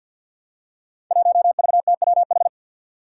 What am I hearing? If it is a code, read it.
Morse code, heard as a keyed tone, about 52 wpm, 700 Hz. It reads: 1VTWH
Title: Five Challenging Letter-Number Combinations 50wpm